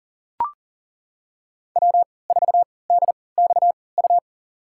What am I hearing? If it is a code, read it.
Morse code, heard as a keyed tone, about 40 wpm, spaced as for 23 wpm, 700 Hz. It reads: W4DXU
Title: USA Call Signs - 5 Characters - 3x Character Spacing - 40wpm